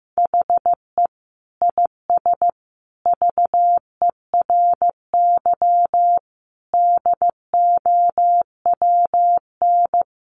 Morse code, heard as a keyed tone, about 15 wpm, 700 Hz. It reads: HE IS VERY DOWN
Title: Sentences from Top 1500 Words 15wpm